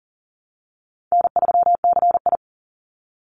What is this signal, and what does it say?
Morse code, heard as a keyed tone, about 40 wpm, 700 Hz. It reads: N3CI
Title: Easy Call Signs Repeated 2x - 40wpm